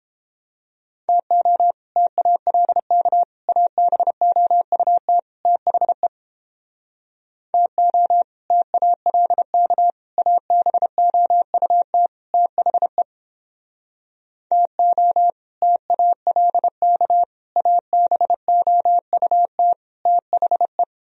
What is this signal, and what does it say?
Morse code, heard as a keyed tone, about 29 wpm, 700 Hz. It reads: TO TALK ABOUT THE TO TALK ABOUT THE TO TALK ABOUT THE